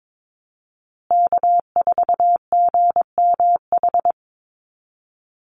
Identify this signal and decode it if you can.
Morse code, heard as a keyed tone, about 22 wpm, 700 Hz. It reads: K4GMH
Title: Call Signs Repeated 2x - 22wpm